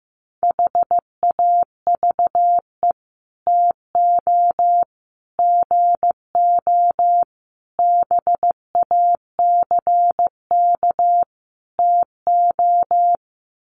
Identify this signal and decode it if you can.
Morse code, heard as a keyed tone, about 15 wpm, 700 Hz. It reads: HAVE TO GO BACK TO
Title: Sets of 5 Words Encore2 15wpm